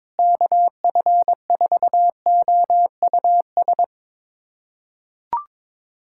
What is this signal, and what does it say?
Morse code, heard as a keyed tone, about 21 wpm, 700 Hz. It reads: KF4OUS
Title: USA Callsign DE Callsign - 6 Characters - 22wpm